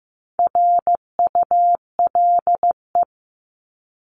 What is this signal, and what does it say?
Morse code, heard as a keyed tone, about 15 wpm, 700 Hz. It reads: RULE
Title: ES:S Suffix - Top 100 Words with Base Words 15wpm